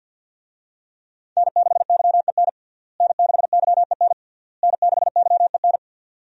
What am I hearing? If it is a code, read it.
Morse code, heard as a keyed tone, about 53 wpm, 700 Hz. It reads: N6YEN N6YEN N6YEN
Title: USA Call Signs - 5 Characters - Repeated 3x - 50wpm